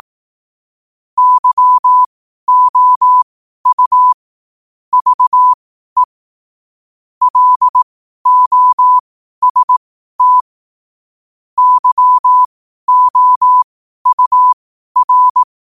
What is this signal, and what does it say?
Morse code, heard as a keyed tone, about 18 wpm, spaced as for 12 wpm, 1000 Hz. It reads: YOU VE LOST YOUR
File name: 12 wpm